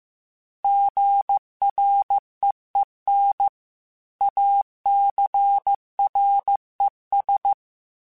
Morse code, heard as a keyed tone, about 15 wpm, 800 Hz. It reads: GREEN ACRES